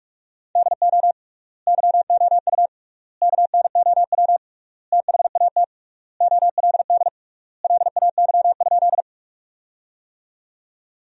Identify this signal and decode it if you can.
Morse code, heard as a keyed tone, about 46 wpm, 700 Hz. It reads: DO YOU KNOW THAT OLD LAY?
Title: 3x Word-Spacing - Sentences from Top 400 Words - 45wpm